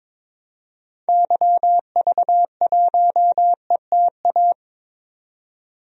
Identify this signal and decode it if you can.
Morse code, heard as a keyed tone, about 22 wpm, 700 Hz. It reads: YV1ETA
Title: DXCC-100 Call Signs - 6 Characters - 22wpm